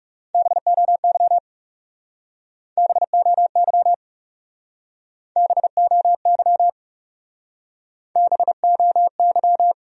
Morse code, heard as a keyed tone, about 37 wpm, 700 Hz. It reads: BOY BOY BOY BOY
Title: Speed-Racing Top 200 Words 30wpm